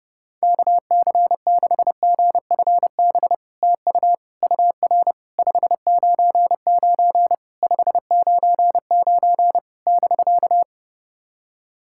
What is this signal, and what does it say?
Morse code, heard as a keyed tone, about 31 wpm, 700 Hz. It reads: KC6GFB TU UR 599 599 <BK>
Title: SOTA QSO Practice 30wpm